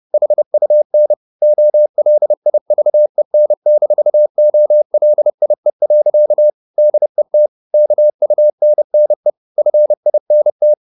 Morse code, heard as a keyed tone, about 30 wpm, 600 Hz. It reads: HUN OLIVEN-OLIE. DET KUNNE FINT
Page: Ferie 30 wpm